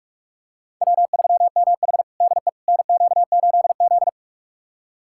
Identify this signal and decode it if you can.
Morse code, heard as a keyed tone, about 46 wpm, 700 Hz. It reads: W3MH DE NQ8Z
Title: USA Callsign DE Callsign - 4 Characters - 45wpm